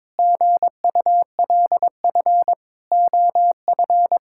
Morse code, heard as a keyed tone, about 22 wpm, 700 Hz. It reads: GULF OF
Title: Pangram Challenge v2 - 22wpm